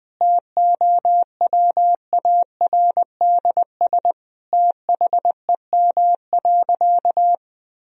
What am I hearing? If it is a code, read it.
Morse code, heard as a keyed tone, about 20 wpm, 700 Hz. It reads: TOWARDS THEM.
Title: Sentences from Top 1000 Words - Review Part 2 - 20wpm